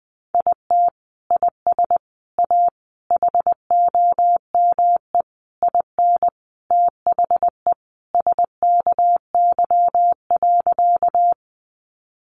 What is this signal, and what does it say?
Morse code, heard as a keyed tone, about 20 wpm, 700 Hz. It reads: IT IS A HOME IN THE SKY.
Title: Sentences from Top 1200 Words 20wpm